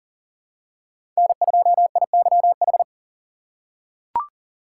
Morse code, heard as a keyed tone, about 40 wpm, 700 Hz. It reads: N2IYH
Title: USA Call Signs - 5 Characters - 4x Character Spacing - 40wpm